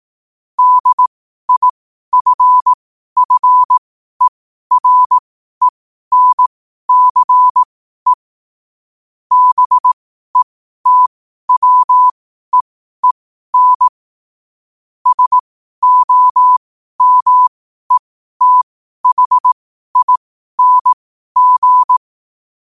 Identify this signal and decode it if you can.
Morse code, heard as a keyed tone, about 18 wpm, spaced as for 12 wpm, 1000 Hz. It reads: DIFFERENCE BETWEEN SOMETHING
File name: qotdfiles / 12wpm